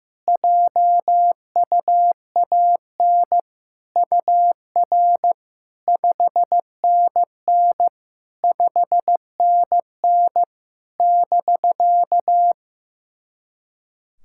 Morse code, heard as a keyed tone, about 15 wpm, 700 Hz. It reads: JUAN UR 5NN 5NN <BK>
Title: SOTA QSO Practice 15wpm